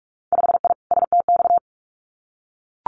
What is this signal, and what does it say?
Morse code, heard as a keyed tone, about 46 wpm, 700 Hz. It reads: 5I STX
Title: Winter Field Day Exchanges Encore 45wpm